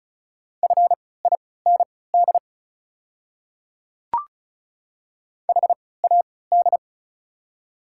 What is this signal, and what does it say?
Morse code, heard as a keyed tone, about 34 wpm, spaced as for 20 wpm, 700 Hz. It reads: FIND HAD
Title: 3x Character Spacing - Top 300 Words - 35wpm